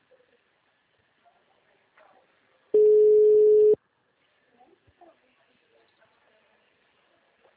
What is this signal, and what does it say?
Ringback tone (Europe, 425 Hz), with speech.